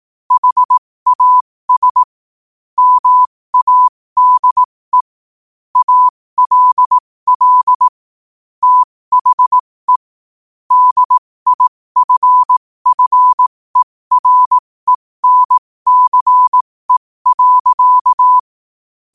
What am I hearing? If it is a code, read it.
Morse code, heard as a keyed tone, about 18 wpm, spaced as for 15 wpm, 1000 Hz. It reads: HAS MADE ALL THE DIFFERENCE.
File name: qotdfiles / 15wpm